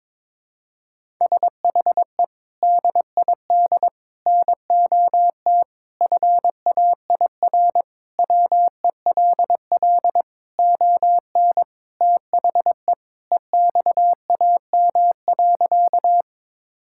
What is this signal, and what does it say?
Morse code, heard as a keyed tone, about 22 wpm, 700 Hz. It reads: SHE DID NOT FAIR WELL ON THE EXAM.
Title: Sentences from Top 1000 Words - Review Part 5 - 22wpm